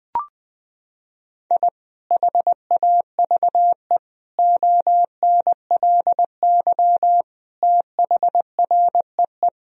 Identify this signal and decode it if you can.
Morse code, heard as a keyed tone, about 21 wpm, 700 Hz. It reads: I HAVE ONLY THREE
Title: Sentences from Top 1000 Words - Review Part 1 - 20wpm